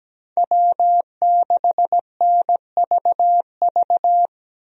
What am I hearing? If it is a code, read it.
Morse code, heard as a keyed tone, about 17 wpm, 700 Hz. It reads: W6NVV
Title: USA Call Signs - 5 Characters - Repeated 3x - 17wpm